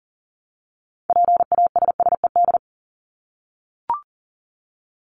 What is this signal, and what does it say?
Morse code, heard as a keyed tone, about 40 wpm, 700 Hz. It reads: PASSED
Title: ED Suffix - Top 100 Words with Base Words 40wpm